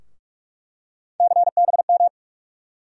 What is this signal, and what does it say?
Morse code, heard as a keyed tone, about 44 wpm, 700 Hz. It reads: CBM
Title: USA Call Sign Postfixes - 3 Letters 45wpm